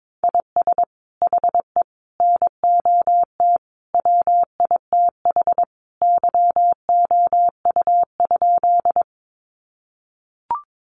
Morse code, heard as a keyed tone, about 22 wpm, 700 Hz. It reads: IS HE NOT WITH YOU?
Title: Sentences from Top 600 Words 22wpm